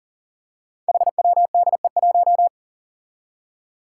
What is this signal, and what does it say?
Morse code, heard as a keyed tone, about 41 wpm, 700 Hz. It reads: HWDE1
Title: Five Challenging Letter-Number Combinations 40wpm